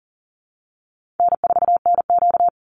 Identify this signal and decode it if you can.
Morse code, heard as a keyed tone, about 40 wpm, 700 Hz. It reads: N4NQ
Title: Easy Call Signs Repeated 2x - 40wpm